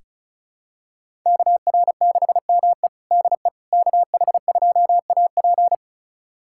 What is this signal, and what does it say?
Morse code, heard as a keyed tone, about 34 wpm, 700 Hz. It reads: KR6ME DE KH2AP
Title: USA Callsign DE Callsign - 5 Characters - 35wpm